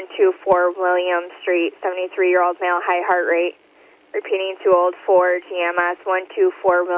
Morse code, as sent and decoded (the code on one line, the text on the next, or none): none